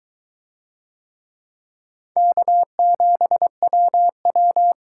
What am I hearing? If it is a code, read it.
Morse code, heard as a keyed tone, about 21 wpm, 700 Hz. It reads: K7WW